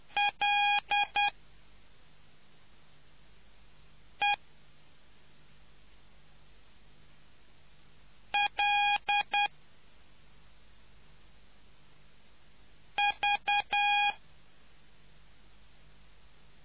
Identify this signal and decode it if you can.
Morse code, heard as a keyed tone, about 10 wpm, spaced as for 2 wpm, 820 Hz. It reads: LELV